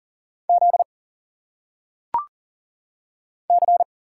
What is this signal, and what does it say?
Morse code, heard as a keyed tone, about 40 wpm, 700 Hz. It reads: Z C
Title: Single Letters - Phonetic Pronunciation 40wpm